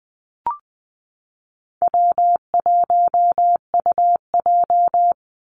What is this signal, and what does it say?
Morse code, heard as a keyed tone, about 20 wpm, 700 Hz. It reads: W1UJ
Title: Easy Call Signs - 20wpm